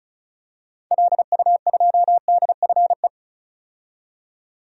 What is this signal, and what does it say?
Morse code, heard as a keyed tone, about 36 wpm, 700 Hz. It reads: LU2DFE
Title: DXCC-100 Call Signs - 6 Characters - 35wpm